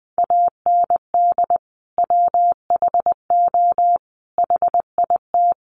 Morse code, heard as a keyed tone, about 20 wpm, 700 Hz. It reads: AND WHO HIT
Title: Sentences from Top 1200 Words 20wpm